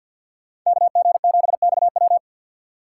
Morse code, heard as a keyed tone, about 52 wpm, 690 Hz. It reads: KG7XW
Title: USA Call Signs - 5 Characters - 3x Character Spacing - 50wpm